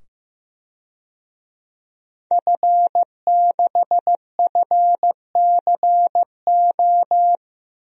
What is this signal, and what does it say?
Morse code, heard as a keyed tone, about 16 wpm, 700 Hz. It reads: F6FCO
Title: DXCC-100 Call Signs - 5 Characters - 15wpm